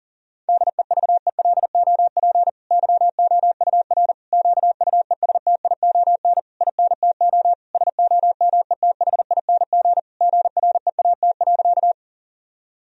Morse code, heard as a keyed tone, about 40 wpm, 700 Hz. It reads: DEVELOP YOUR QUESTION INTO SOMETHING GREAT.
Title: Sentences from Top 1000 Words - Review Part 2 - 40wpm